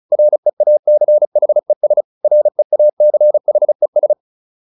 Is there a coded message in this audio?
Yes: Morse code from 0.1 to 4.1 s, heard as a keyed tone, about 34 wpm, 600 Hz. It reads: REACHES REACHES